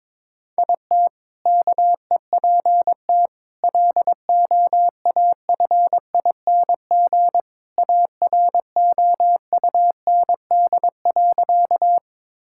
Morse code, heard as a keyed tone, about 22 wpm, 700 Hz. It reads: IT KEPT LOAFING AROUND.
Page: morse-code-ninja-practice-22wpm